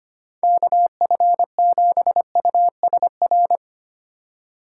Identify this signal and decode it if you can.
Morse code, heard as a keyed tone, about 25 wpm, 700 Hz. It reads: KF7USR
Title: Speed Racing - USA 2x3 Call Signs 25wpm